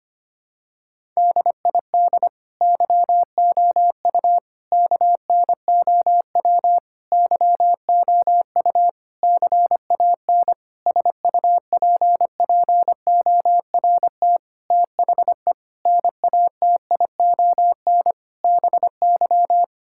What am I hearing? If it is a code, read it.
Morse code, heard as a keyed tone, about 25 wpm, 700 Hz. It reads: DID YOU KNOW YOU CAN SUPPORT THE NATION BY
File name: Sets of 9 Words - 25wpm